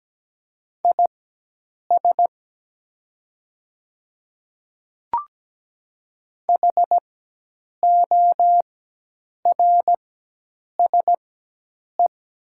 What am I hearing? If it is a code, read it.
Morse code, heard as a keyed tone, about 17 wpm, spaced as for 8 wpm, 700 Hz. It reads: IS HORSE
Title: 4x Character Spacing - Top 300 Words - 17wpm